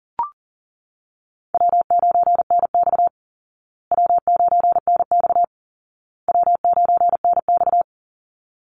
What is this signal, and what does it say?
Morse code, heard as a keyed tone, about 40 wpm, 700 Hz. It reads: W9NX W9NX W9NX
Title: Easy Call Signs Repeated 3x - 40wpm